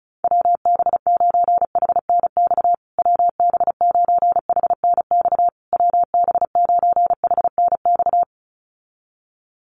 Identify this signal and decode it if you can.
Morse code, heard as a keyed tone, about 34 wpm, 700 Hz. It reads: WB9HNX WB9HNX WB9HNX
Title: USA Call Signs Repeated 3x 35wpm